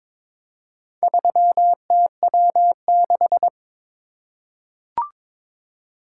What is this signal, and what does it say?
Morse code, heard as a keyed tone, about 22 wpm, 700 Hz. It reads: 3TW6
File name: practice / 22wpm / Four Letter-Number Combinations - 22wpm